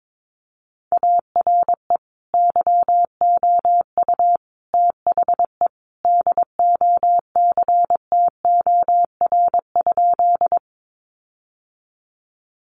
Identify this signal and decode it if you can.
Morse code, heard as a keyed tone, about 22 wpm, 700 Hz. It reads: ARE YOU THE DOCTOR?
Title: Sentences from Top 1500 Words 22wpm